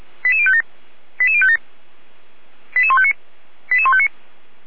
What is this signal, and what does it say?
Five-tone selective call (ZVEI), heard as quick runs of five tones, tones of about 70 ms. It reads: 88057 88057 88159 88159